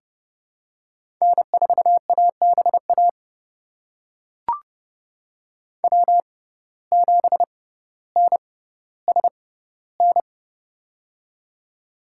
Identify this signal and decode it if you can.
Morse code, heard as a keyed tone, about 30 wpm, spaced as for 15 wpm, 700 Hz. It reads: N4ABA W7NSN
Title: USA Call Signs - 5 Characters - 6x Character Spacing - 30wpm